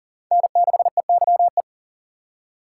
Morse code, heard as a keyed tone, about 40 wpm, 700 Hz. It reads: N6EYE
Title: USA Call Signs - 5 Characters - 4x Character Spacing - 40wpm